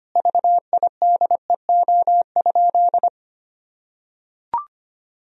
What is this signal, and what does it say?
Morse code, heard as a keyed tone, about 25 wpm, 700 Hz. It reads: VIDEO?
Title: Sentences from Top 1000 Words - Review Part 4 - 25wpm